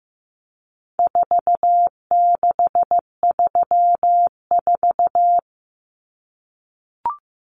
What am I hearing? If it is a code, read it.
Morse code, heard as a keyed tone, about 15 wpm, 700 Hz. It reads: 4634
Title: Numbers - 4-Digit 15wpm